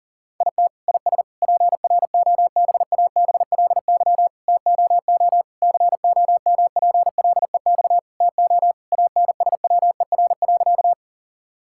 Morse code, heard as a keyed tone, about 37 wpm, 700 Hz. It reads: IT IS PROBABLY TOO COMPLEX TO ANSWER.
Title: Sentences from Top 1000 Words - Review Part 6 - 40wpm